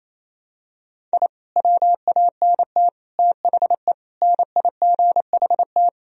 Morse code, heard as a keyed tone, about 28 wpm, 700 Hz. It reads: I WANT THE NIGHT
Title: Sentences from Top 1000 Words - Review Part 1 - 28wpm